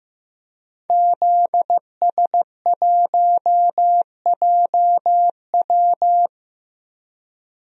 Morse code, heard as a keyed tone, about 15 wpm, 700 Hz. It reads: ZS1JW
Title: DXCC-100 Call Signs - 5 Characters - Repeated 3x - 15wpm